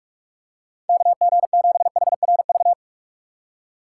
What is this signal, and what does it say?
Morse code, heard as a keyed tone, about 46 wpm, 700 Hz. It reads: KG7HRV
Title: USA Call Signs - 2x Character Spacing - 45wpm